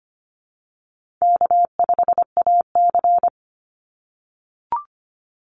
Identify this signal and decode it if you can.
Morse code, heard as a keyed tone, about 25 wpm, 700 Hz. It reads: K5AC
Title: USA Call Signs 25wpm